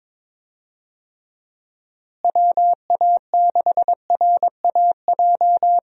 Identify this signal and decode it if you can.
Morse code, heard as a keyed tone, about 22 wpm, 700 Hz. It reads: WA6RAJ
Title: USA Call Signs - 6 Characters - Repeated 2x - 22wpm